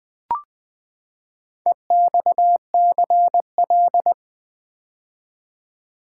Morse code, heard as a keyed tone, about 20 wpm, 700 Hz. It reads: EXCL
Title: Build A Word - 4 Letters At A Time - 11-Letter Words - 20wpm